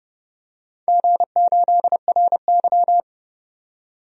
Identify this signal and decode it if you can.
Morse code, heard as a keyed tone, about 30 wpm, 700 Hz. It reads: G8RY
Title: DXCC-100 Call Signs - 4 Characters - 30wpm